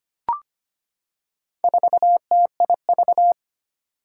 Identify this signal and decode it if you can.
Morse code, heard as a keyed tone, about 25 wpm, 700 Hz. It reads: 4TIV
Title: Four Challenging Letter-Number Combinations 25wpm